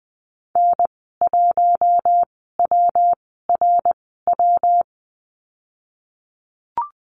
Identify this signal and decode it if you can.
Morse code, heard as a keyed tone, about 20 wpm, spaced as for 15 wpm, 700 Hz. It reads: N1WRW
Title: 147 Lesson 24g - Letter y - Call Signs 20wpm